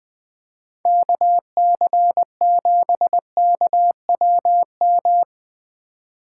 Morse code, heard as a keyed tone, about 22 wpm, 700 Hz. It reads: KC7KWM
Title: Speed Racing - USA 2x3 Call Signs 20wpm